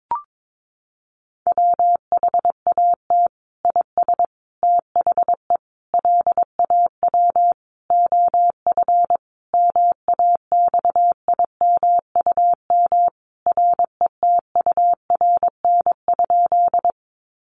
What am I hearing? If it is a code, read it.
Morse code, heard as a keyed tone, about 22 wpm, 700 Hz. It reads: WHAT IS THE LAW OF MAXIMUM RETURN?
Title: Sentences from Top 1300 Words 22wpm